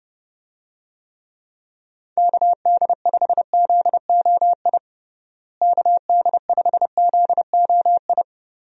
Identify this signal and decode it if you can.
Morse code, heard as a keyed tone, about 31 wpm, 700 Hz. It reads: KD5ZOI KD5ZOI
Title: USA Call Signs - 6 Characters - Repeated 3x - 30wpm